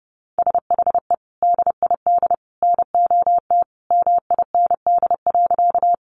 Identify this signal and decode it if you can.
Morse code, heard as a keyed tone, about 30 wpm, 700 Hz. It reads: SHE DID NOT MIND.
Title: Sentences from Top 600 Words 30wpm